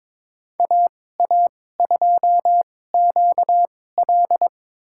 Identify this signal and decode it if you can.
Morse code, heard as a keyed tone, about 22 wpm, spaced as for 16 wpm, 700 Hz. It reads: AA2QL
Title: USA Call Signs - 5 Characters - 2x Character Spacing - 22wpm